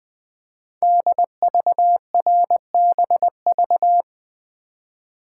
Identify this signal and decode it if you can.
Morse code, heard as a keyed tone, about 20 wpm, 700 Hz. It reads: DVRBV
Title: Code Groups - 5 Letters - 20wpm